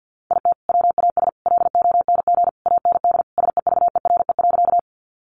Morse code, heard as a keyed tone, about 50 wpm, 700 Hz. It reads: IT WAS LONG AND SEVERE.
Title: Sentences from Top 1400 Words 50wpm